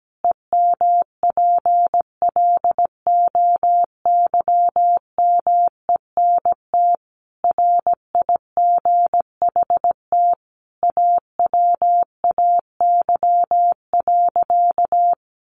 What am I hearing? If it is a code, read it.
Morse code, heard as a keyed tone, about 17 wpm, 700 Hz. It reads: EMPLOYMENT RIGHT AWAY.
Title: Sentences from Top 1500 Words 17wpm